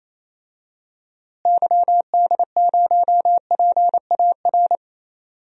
Morse code, heard as a keyed tone, about 28 wpm, 700 Hz. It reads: YD0PAR
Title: DXCC-100 Call Signs - 6 Characters - Repeated 2x - 28wpm